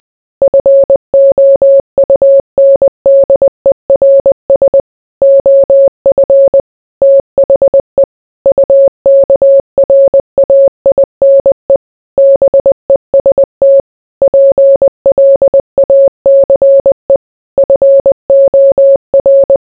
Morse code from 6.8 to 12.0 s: THE UKRAINE